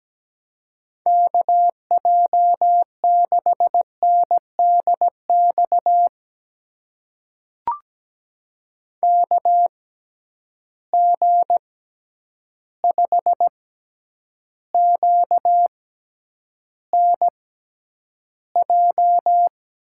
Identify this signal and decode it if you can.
Morse code, heard as a keyed tone, about 17 wpm, spaced as for 9 wpm, 700 Hz. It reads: KJ6NDX KG5QNJ
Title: USA Call Signs - 6 Characters - 6x Character Spacing - 17wpm